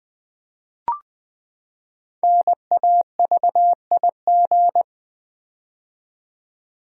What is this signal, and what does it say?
Morse code, heard as a keyed tone, about 20 wpm, 700 Hz. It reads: NAVIG
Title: Build A Word - 5 Letters At A Time - 10-Letter Words - 20wpm